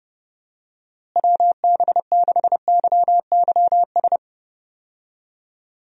Morse code, heard as a keyed tone, about 30 wpm, 700 Hz. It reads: WB6YYS
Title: USA Call Signs - 6 Characters - Repeated 2x - 30wpm